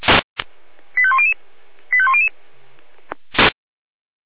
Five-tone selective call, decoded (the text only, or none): 85100 85100